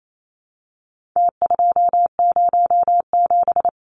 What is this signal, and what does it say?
Morse code, heard as a keyed tone, about 28 wpm, 700 Hz. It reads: T207
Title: Numbers - Contest Style - 4-Digit 28wpm